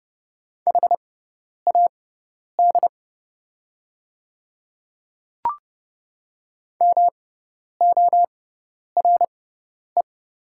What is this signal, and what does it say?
Morse code, heard as a keyed tone, about 30 wpm, spaced as for 10 wpm, 700 Hz. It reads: HAD MORE
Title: 6x Character Spacing - Top 100 Words - 30wpm